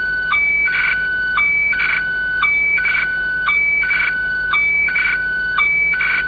SSTV (slow-scan television), Scottie DX.
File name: scanner / hffax / scottdx